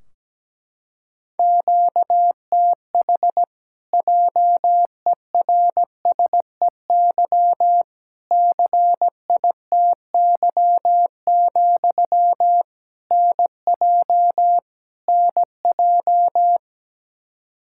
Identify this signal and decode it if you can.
Morse code, heard as a keyed tone, about 17 wpm, 700 Hz. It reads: QTH JERSEY CITY, NJ NJ
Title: Practice QSO - Exchange 1 - Beginning - 17wpm